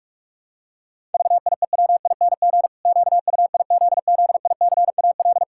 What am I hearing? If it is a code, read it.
Morse code, heard as a keyed tone, about 46 wpm, 700 Hz. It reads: VIEWING QUIZZICAL
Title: Pangram Challenge v2 - 45wpm